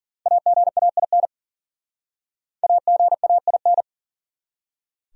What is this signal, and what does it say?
Morse code, heard as a keyed tone, about 52 wpm, 700 Hz. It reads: AGAIN AGAIN